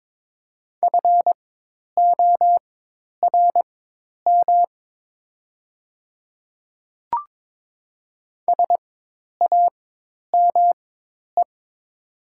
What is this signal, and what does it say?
Morse code, heard as a keyed tone, about 22 wpm, spaced as for 10 wpm, 700 Hz. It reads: FORM SAME
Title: 4x Character Spacing - Top 300 Words - 22wpm